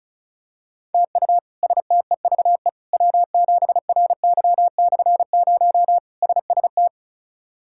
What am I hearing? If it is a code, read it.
Morse code, heard as a keyed tone, about 36 wpm, 700 Hz. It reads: TU STEVE W7RY/0 SST
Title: K1USN SST QSO Practice - 35wpm